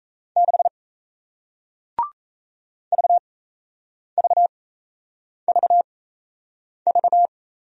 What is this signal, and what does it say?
Morse code, heard as a keyed tone, about 38 wpm, spaced as for 8 wpm, 700 Hz. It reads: 6 VVVV